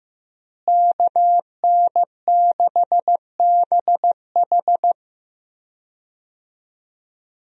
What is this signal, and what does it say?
Morse code, heard as a keyed tone, about 15 wpm, 700 Hz. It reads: KN6BH
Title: USA Call Signs - 5 Characters - Repeated 2x - 15wpm